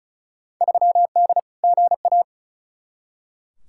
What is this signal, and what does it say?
Morse code, heard as a keyed tone, about 36 wpm, 690 Hz. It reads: 3D GA
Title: Field Day Exchanges Encore - 35wpm